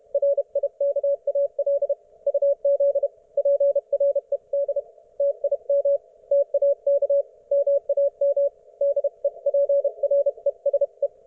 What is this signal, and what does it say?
Morse code, heard as a keyed tone, about 32 wpm, 560 Hz. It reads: RIKAL UZ PRED TIM TAK MAM DEPRESE